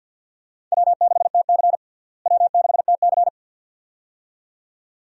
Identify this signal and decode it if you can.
Morse code, heard as a keyed tone, about 52 wpm, 700 Hz. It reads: W6TC W6TC
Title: USA Call Signs - 4 Characters - Repeated 2x - 50wpm